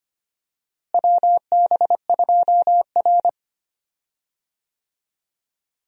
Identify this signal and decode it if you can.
Morse code, heard as a keyed tone, about 25 wpm, 700 Hz. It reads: WB2R
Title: USA Call Signs - 4 Characters - Repeated 2x - 25wpm